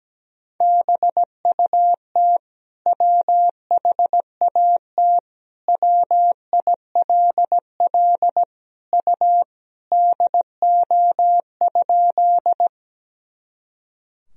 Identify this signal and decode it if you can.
Morse code, heard as a keyed tone, about 17 wpm, 700 Hz. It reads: BUT WHAT WILL U DO?
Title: Sentences from Top 100 Words and Abbreviations 17wpm